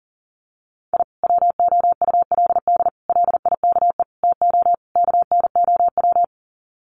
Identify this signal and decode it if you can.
Morse code, heard as a keyed tone, about 40 wpm, 700 Hz. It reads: I WOULD LIKE TO KNOW
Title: Sets of 5 Words Encore 40wpm